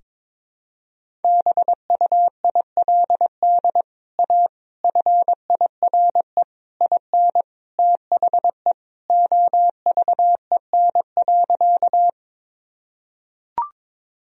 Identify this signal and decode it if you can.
Morse code, heard as a keyed tone, about 21 wpm, 700 Hz. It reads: BUILD A FIRE IN THE OVEN.
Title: Sentences from Top 1000 Words - Review Part 4 - 22wpm